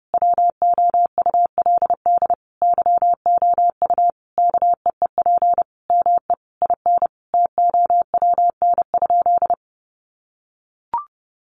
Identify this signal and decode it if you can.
Morse code, heard as a keyed tone, about 30 wpm, 700 Hz. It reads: WOULD YOU KEEP ME IN TOWN?
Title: Sentences from Top 500 Words 30wpm